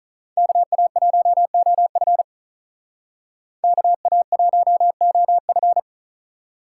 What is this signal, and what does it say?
Morse code, heard as a keyed tone, about 41 wpm, 700 Hz. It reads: KA1OF KA1OF
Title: Speed Racing - USA 2x2 Call Signs 35wpm